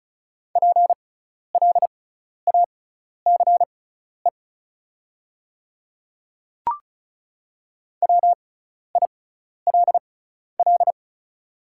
Morse code, heard as a keyed tone, about 34 wpm, spaced as for 12 wpm, 700 Hz. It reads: PLACE WILL
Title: 6x Character Spacing - Top 200 Words - 35wpm